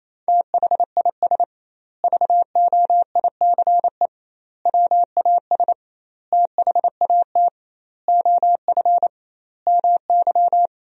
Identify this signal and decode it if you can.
Morse code, heard as a keyed tone, about 28 wpm, 700 Hz. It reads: THIS VOICE WAS THAT OF MY
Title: 2x Word-Spacing - Sentences from Top 500 Words - 28wpm